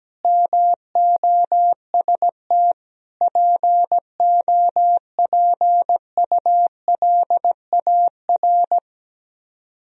Morse code, heard as a keyed tone, about 17 wpm, 700 Hz. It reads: MOST POPULAR
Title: Speed-Racing - Sets of 5 Words 17wpm